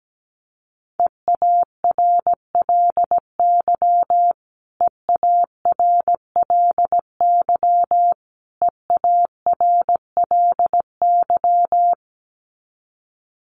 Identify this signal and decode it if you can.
Morse code, heard as a keyed tone, about 17 wpm, 700 Hz. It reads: EARLY EARLY EARLY
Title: Top 500 Words Repeated 17wpm